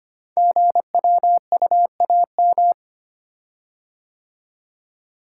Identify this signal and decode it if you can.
Morse code, heard as a keyed tone, about 25 wpm, 700 Hz. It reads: GWUAM
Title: Code Groups - 5 Letters - 25wpm